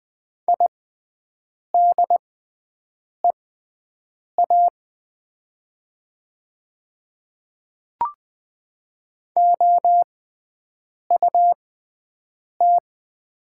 Morse code, heard as a keyed tone, about 20 wpm, spaced as for 7 wpm, 700 Hz. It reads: IDEA OUT
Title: 6x Character Spacing - Top 500 Words - 20wpm